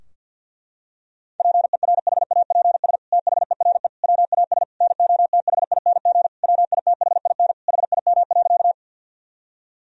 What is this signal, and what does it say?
Morse code, heard as a keyed tone, about 50 wpm, 700 Hz. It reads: PERHAPS THERE WAS NOTHING WITHIN HIM.
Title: Sentences from Top 1000 Words - Review Part 4 - 50wpm